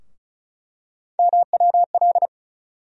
Morse code, heard as a keyed tone, about 36 wpm, 700 Hz. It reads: MWL
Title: USA Call Sign Postfixes - 3 Letters 35wpm